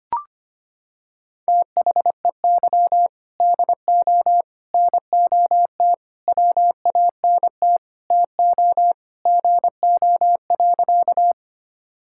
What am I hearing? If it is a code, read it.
Morse code, heard as a keyed tone, about 25 wpm, 700 Hz. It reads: THEY DO NOT WANT TO GO.
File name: practice / 25wpm / Sentences with 4-Word Phrases - 25wpm